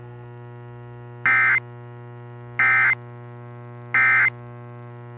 Emergency Alert System (SAME) tones, heard as harsh buzzing bursts. It is heard from 1.1 to 4.4 s.